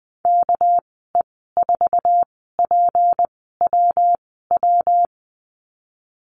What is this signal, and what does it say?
Morse code, heard as a keyed tone, about 20 wpm, spaced as for 15 wpm, 700 Hz. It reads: KE4PWW
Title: 231 Lesson 37g - Letter x - Call Signs 20wpm